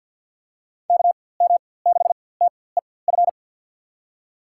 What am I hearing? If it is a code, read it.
Morse code, heard as a keyed tone, about 53 wpm, spaced as for 24 wpm, 700 Hz. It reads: KM6TEF
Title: USA Call Signs - 6 Characters - 4x Character Spacing - 50wpm